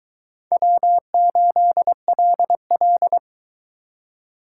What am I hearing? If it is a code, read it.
Morse code, heard as a keyed tone, about 22 wpm, 700 Hz. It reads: W8LL